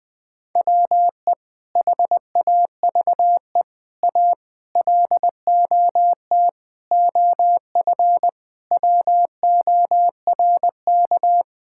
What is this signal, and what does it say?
Morse code, heard as a keyed tone, about 20 wpm, 700 Hz. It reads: WE HAVE A LOT OF WORK
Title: Sets of 8 Words - 20wpm